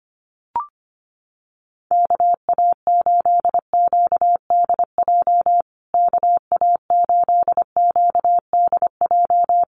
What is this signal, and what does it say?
Morse code, heard as a keyed tone, about 25 wpm, 700 Hz. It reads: KA8QDJ KA8QDJ
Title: USA Call Signs Repeated 3x 25wpm